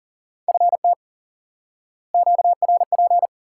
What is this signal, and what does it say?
Morse code, heard as a keyed tone, about 40 wpm, 700 Hz. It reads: FT QRP
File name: IQE Course 3 - Lesson 89 - YAGI - Two Elements - 40wpm